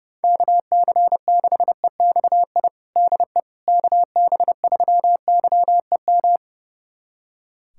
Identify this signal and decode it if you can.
Morse code, heard as a keyed tone, about 32 wpm, 700 Hz. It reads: KC6EXI DE KB3YEM
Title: USA Callsign DE Callsign 30wpm